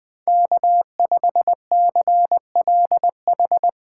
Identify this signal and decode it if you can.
Morse code, heard as a keyed tone, about 20 wpm, 700 Hz. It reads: K5CLH
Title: Speed Racing - USA 1x3 Call Signs 17wpm